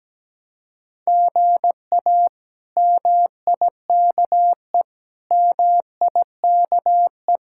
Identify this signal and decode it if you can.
Morse code, heard as a keyed tone, about 17 wpm, 700 Hz. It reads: GA MIKE MIKE
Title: K1USN SST QSO Practice - 17wpm